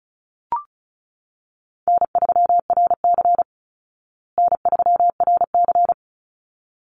Morse code, heard as a keyed tone, about 34 wpm, 700 Hz. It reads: N3RC N3RC
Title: Easy Call Signs Repeated 2x - 35wpm